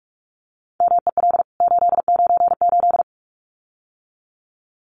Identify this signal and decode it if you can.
Morse code, heard as a keyed tone, about 44 wpm, 700 Hz. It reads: MEL 898
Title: CWT Exchange 45wpm